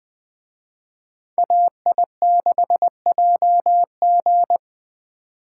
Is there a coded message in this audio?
Yes: Morse code from 1.4 to 4.6 s, heard as a keyed tone, about 20 wpm, 700 Hz. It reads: AI6JG